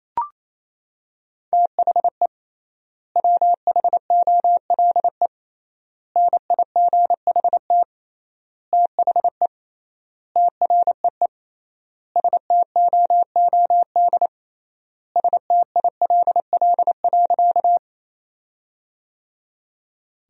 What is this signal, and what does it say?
Morse code, heard as a keyed tone, about 28 wpm, 700 Hz. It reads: THE WHOLE NIGHT THE TREE STOOD STILL.
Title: 3x Word-Spacing - Sentences from Top 500 Words - 28wpm